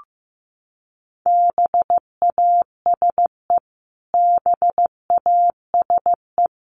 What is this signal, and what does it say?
Morse code, heard as a keyed tone, about 14 wpm, 700 Hz. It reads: BASE BASE
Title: Top 500 Words Repeated 15wpm